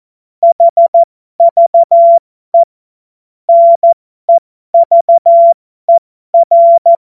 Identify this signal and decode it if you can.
Morse code, heard as a keyed tone, about 14 wpm, spaced as for 12 wpm, 680 Hz. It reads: HVE NEVER